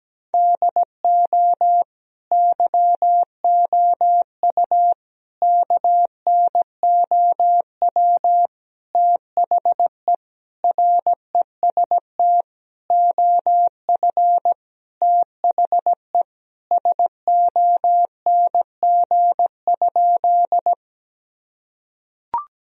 Morse code, heard as a keyed tone, about 17 wpm, 700 Hz. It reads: DO YOU KNOW THE REST OF THE SONG?
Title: Sentences from Top 1000 Words - Review Part 2 - 17wpm